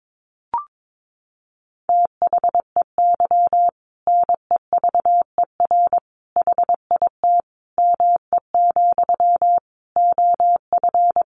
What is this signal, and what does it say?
Morse code, heard as a keyed tone, about 22 wpm, 700 Hz. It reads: THEY NEVER HIT ME, OF
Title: Sentences from Top 1200 Words 22wpm